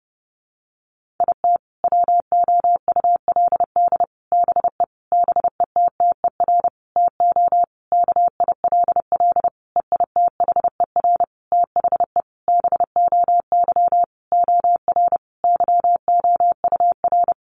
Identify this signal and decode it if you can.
Morse code, heard as a keyed tone, about 30 wpm, 700 Hz. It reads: IT WOULD BE BETTER TO KILL EITHER THE BOY OR YOUR
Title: Sentences from Top 1500 Words 30wpm